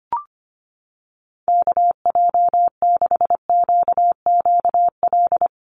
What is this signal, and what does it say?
Morse code, heard as a keyed tone, about 25 wpm, 700 Hz. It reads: KJ6QQL
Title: USA Call Signs Repeated 3x 25wpm